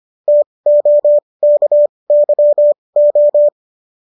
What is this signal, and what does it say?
Morse code, heard as a keyed tone, about 25 wpm, spaced as for 20 wpm, 590 Hz. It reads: TOKYO